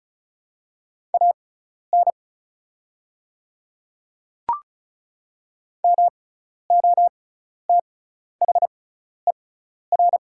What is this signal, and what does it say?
Morse code, heard as a keyed tone, about 35 wpm, spaced as for 12 wpm, 690 Hz. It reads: AN MOTHER
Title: 6x Character Spacing - Top 200 Words - 35wpm